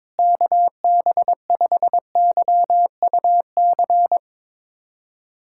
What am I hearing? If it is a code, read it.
Morse code, heard as a keyed tone, about 22 wpm, 700 Hz. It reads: KB5YUC
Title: USA Callsign DE Callsign 22wpm